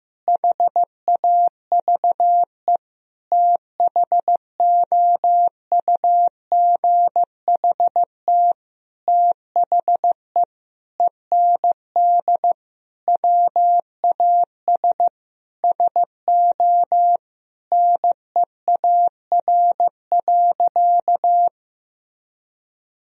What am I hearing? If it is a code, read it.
Morse code, heard as a keyed tone, about 15 wpm, 700 Hz. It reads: HAVE THOUGHT THE END WAS SO NEAR.
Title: Sentences from Top 1000 Words - Review Part 1 - 15wpm